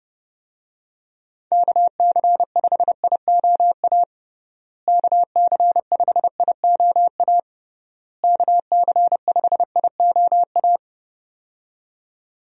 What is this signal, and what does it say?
Morse code, heard as a keyed tone, about 30 wpm, 700 Hz. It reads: KC5IOA KC5IOA KC5IOA
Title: USA Call Signs - 6 Characters - Repeated 3x - 30wpm